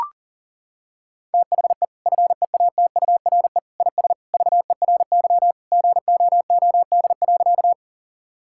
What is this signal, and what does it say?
Morse code, heard as a keyed tone, about 40 wpm, 700 Hz. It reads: THE FEATURE IS VERY GOOD.
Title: Sentences from Top 1000 Words - Review Part 5 - 40wpm